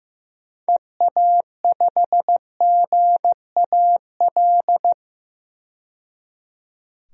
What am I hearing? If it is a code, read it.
Morse code, heard as a keyed tone, about 16 wpm, 700 Hz. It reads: EA5GAL
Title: DXCC-100 Call Signs - 6 Characters - 15wpm